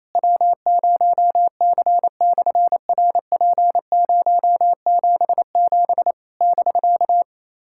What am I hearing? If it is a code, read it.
Morse code, heard as a keyed tone, about 28 wpm, 700 Hz. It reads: W0C/RP077 <BK>
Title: SOTA QSO Practice 28wpm